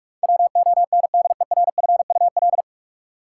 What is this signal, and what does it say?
Morse code, heard as a keyed tone, about 46 wpm, 700 Hz. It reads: WONDERFUL
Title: Sentences from Top 1100 Words - New Word Drill - 45wpm